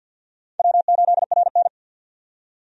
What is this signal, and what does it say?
Morse code, heard as a keyed tone, about 50 wpm, 700 Hz. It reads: W8RN
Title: USA Call Signs - 4 Characters - 50wpm